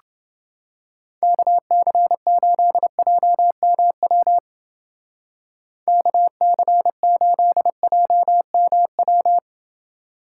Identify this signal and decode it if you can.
Morse code, heard as a keyed tone, about 29 wpm, 700 Hz. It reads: KC8JMW KC8JMW